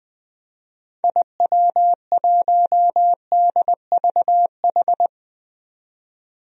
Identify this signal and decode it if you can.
Morse code, heard as a keyed tone, about 20 wpm, 700 Hz. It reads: IW1DVH
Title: DXCC-100 Call Signs - 6 Characters - 20wpm